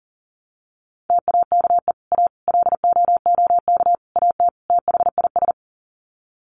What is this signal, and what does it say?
Morse code, heard as a keyed tone, about 40 wpm, 700 Hz. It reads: TAKE A LOOK AT THIS
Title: Sets of 5 Words 40wpm